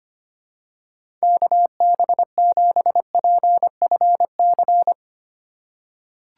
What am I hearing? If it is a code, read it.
Morse code, heard as a keyed tone, about 25 wpm, 700 Hz. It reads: KB7PFC
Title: USA Call Signs - 6 Characters - 3x Character Spacing - 25wpm